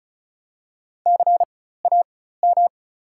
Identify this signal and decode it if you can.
Morse code, heard as a keyed tone, about 36 wpm, spaced as for 16 wpm, 700 Hz. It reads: CAM
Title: 4x Character Spacing - Top 300 Words - 35wpm